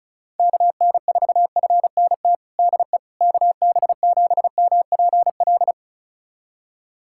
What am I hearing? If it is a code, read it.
Morse code, heard as a keyed tone, about 36 wpm, 700 Hz. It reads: KN4FNT DE KB7MPL
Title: USA Callsign DE Callsign - 6 Characters - 35wpm